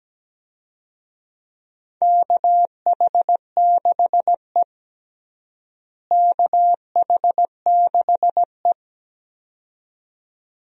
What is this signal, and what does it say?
Morse code, heard as a keyed tone, about 17 wpm, 700 Hz. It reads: KH6E KH6E
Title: DXCC-100 Call Signs - 4 Characters - Repeated 2x - 17wpm